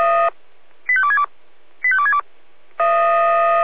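Five-tone selective call (ZVEI), heard as quick runs of five tones, tones of about 70 ms. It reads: 85262 85262
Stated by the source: Five-tone ZVEI selective call received by SDR.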